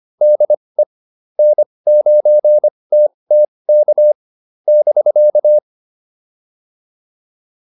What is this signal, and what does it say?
Morse code, heard as a keyed tone, about 25 wpm, spaced as for 20 wpm, 600 Hz. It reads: DE N9TTK <BK>